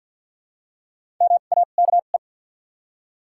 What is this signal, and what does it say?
Morse code, heard as a keyed tone, about 53 wpm, spaced as for 37 wpm, 700 Hz. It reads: MAKE
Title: 2x Character Spacing - Top 500 Words - 50wpm